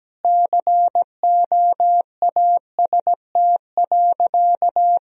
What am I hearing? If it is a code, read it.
Morse code, heard as a keyed tone, about 17 wpm, 700 Hz. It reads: COAST.
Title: Sentences from Top 1000 Words - Review Part 4 - 17wpm